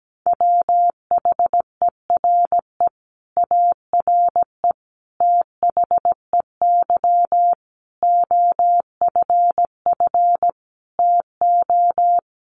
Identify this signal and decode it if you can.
Morse code, heard as a keyed tone, about 17 wpm, 700 Hz. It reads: WHERE ARE THEY OFF TO
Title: Sentences from Top 200 Words 17wpm